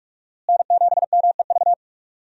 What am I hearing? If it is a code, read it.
Morse code, heard as a keyed tone, about 43 wpm, 700 Hz. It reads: N7MEV
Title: USA Call Signs - 5 Characters - Repeated 2x - 45wpm